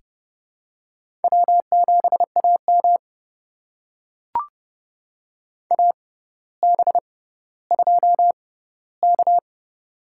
Morse code, heard as a keyed tone, about 30 wpm, spaced as for 15 wpm, 700 Hz. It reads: W7AM AB2K
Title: USA Call Signs - 4 Characters - 6x Character Spacing - 30wpm